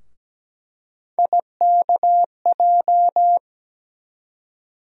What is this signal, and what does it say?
Morse code, heard as a keyed tone, about 17 wpm, 700 Hz. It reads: IKJ
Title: USA Call Sign Postfixes - 3 Letters 17wpm